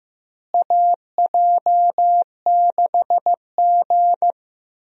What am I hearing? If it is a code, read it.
Morse code, heard as a keyed tone, about 15 wpm, 700 Hz. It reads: AJ6G
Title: USA Call Signs - 4 Characters - Repeated 2x - 15wpm